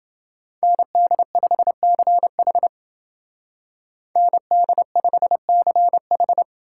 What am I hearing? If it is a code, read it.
Morse code, heard as a keyed tone, about 28 wpm, 700 Hz. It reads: ND5CH ND5CH